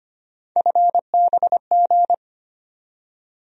Morse code, heard as a keyed tone, about 25 wpm, 700 Hz. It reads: FBG
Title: USA Call Sign Postfixes - 3 Letters 25wpm